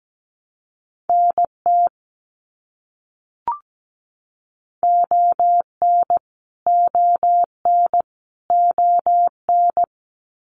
Morse code, heard as a keyed tone, about 17 wpm, 700 Hz. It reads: NT ON ON ON
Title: Canadian Provinces and Territories - Repeated 3x - 17wpm